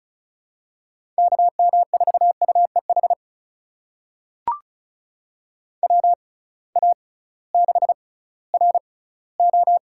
Morse code, heard as a keyed tone, about 36 wpm, spaced as for 19 wpm, 700 Hz. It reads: KM4UEH WA6RO
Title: USA Call Signs - 6 Characters - 6x Character Spacing - 35wpm